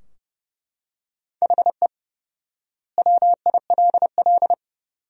Morse code, heard as a keyed tone, about 32 wpm, 700 Hz. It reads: HE WILL
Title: 4x Word-Spacing - Sentences from Top 500 Words - 30wpm